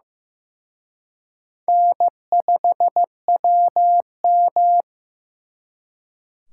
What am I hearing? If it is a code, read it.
Morse code, heard as a keyed tone, about 18 wpm, 700 Hz. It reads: N5WM
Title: Speed Racing - USA 1x2 Call Signs 15wpm